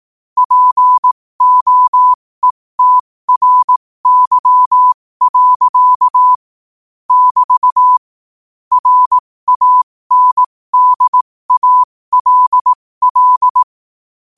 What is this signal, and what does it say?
Morse code, heard as a keyed tone, about 18 wpm, spaced as for 15 wpm, 1000 Hz. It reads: POETRY. <BT> RANDALL
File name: qotdfiles / 15wpm